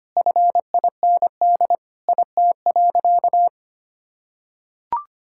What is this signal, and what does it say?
Morse code, heard as a keyed tone, about 25 wpm, 700 Hz. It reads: FIND IT.
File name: practice / 25wpm / Sentences with 4-Word Phrases - 25wpm